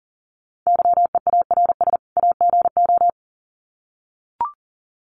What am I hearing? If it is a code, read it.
Morse code, heard as a keyed tone, about 40 wpm, 700 Hz. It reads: YEARS AGO
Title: Sets of 2 Words 40wpm